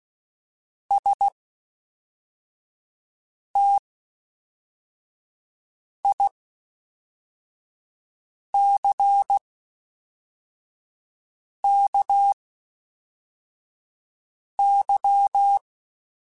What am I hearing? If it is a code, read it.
Morse code, heard as a keyed tone, about 16 wpm, spaced as for 4 wpm, 800 Hz. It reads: STICKY
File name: Misc / Misc-mp3s / WordPairs sp @5